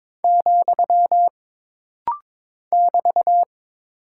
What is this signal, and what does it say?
Morse code, heard as a keyed tone, about 22 wpm, 700 Hz. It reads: , <BT>
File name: practice / 22wpm / Single Punctuation-Prosign - Rapid-Fire - 22wpm